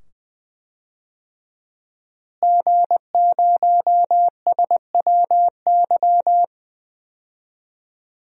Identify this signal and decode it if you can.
Morse code, heard as a keyed tone, about 20 wpm, 700 Hz. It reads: G0SWY
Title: DXCC-100 Call Signs - 5 Characters - Repeated 2x - 20wpm